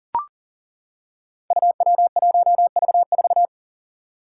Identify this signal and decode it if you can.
Morse code, heard as a keyed tone, about 41 wpm, 700 Hz. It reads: UW1V4
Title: Five Challenging Letter-Number Combinations 40wpm